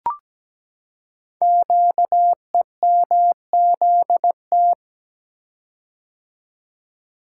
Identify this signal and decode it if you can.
Morse code, heard as a keyed tone, about 17 wpm, 700 Hz. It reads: QEMZT
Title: Code Groups - 5 Letters - 17wpm